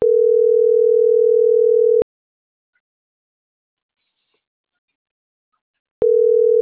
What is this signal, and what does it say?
Ringback tone (North America), 440 + 480 Hz.